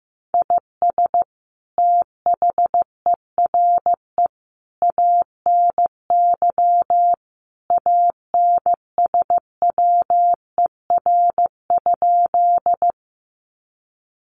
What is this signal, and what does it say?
Morse code, heard as a keyed tone, about 14 wpm, 700 Hz. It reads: IS THERE ANY ANSWER?
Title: Sentences from Top 300 Words 15wpm